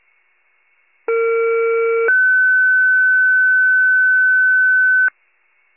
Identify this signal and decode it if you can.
Two-tone page, heard as a one-second tone, then a three-second tone, about 460 then 1500 Hz. It is heard from 1.1 to 5.1 s.